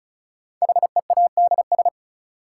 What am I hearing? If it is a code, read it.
Morse code, heard as a keyed tone, about 36 wpm, 700 Hz. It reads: HEADS
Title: Sentences from Top 1000 Words - Review Part 3 - New Word Drill - 35wpm